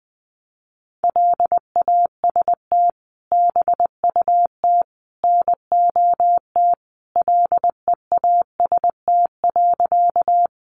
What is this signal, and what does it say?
Morse code, heard as a keyed tone, about 20 wpm, 700 Hz. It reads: LAST BUT NOT LEAST.
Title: Sentences from Top 700 Words 20wpm